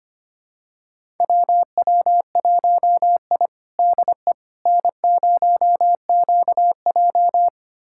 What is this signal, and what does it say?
Morse code, heard as a keyed tone, about 25 wpm, 700 Hz. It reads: WW1I DE N0QJ
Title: USA Callsign DE Callsign - 4 Characters - 25wpm